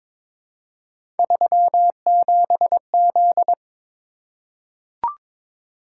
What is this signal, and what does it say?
Morse code, heard as a keyed tone, about 22 wpm, 700 Hz. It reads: 37Z
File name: practice / 22wpm / Three Letter-Number Combinations - 22wpm